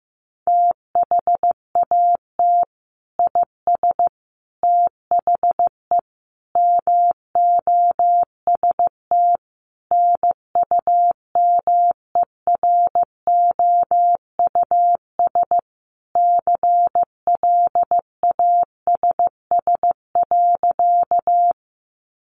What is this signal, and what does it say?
Morse code, heard as a keyed tone, about 15 wpm, 700 Hz. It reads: THAT IS THE MOST NUMEROUS CLASS.
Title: Sentences from Top 1400 Words 15wpm